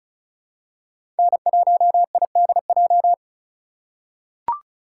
Morse code, heard as a keyed tone, about 34 wpm, 700 Hz. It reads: N1IDJ
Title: USA Call Signs - 5 Characters - 3x Character Spacing - 35wpm